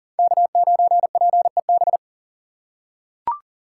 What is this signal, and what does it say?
Morse code, heard as a keyed tone, about 40 wpm, 690 Hz. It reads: K9PEB
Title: USA Call Signs - 8x Character Spacing - 40wpm